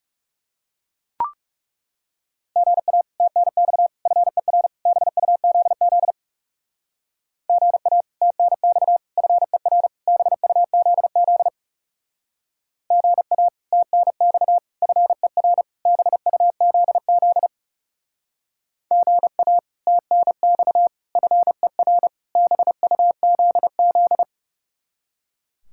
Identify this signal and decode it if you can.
Morse code, heard as a keyed tone, about 36 wpm, 700 Hz. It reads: GA TNX FER BUZZ GA TNX FER BUZZ GA TNX FER BUZZ GA TNX FER BUZZ
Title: Speed-Racing Common QSO Phrases Encore - 30wpm